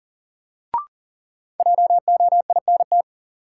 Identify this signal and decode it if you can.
Morse code, heard as a keyed tone, about 40 wpm, 700 Hz. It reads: JOINT
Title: Sentences from Top 1300 Words - New Word Drill Alt - 40wpm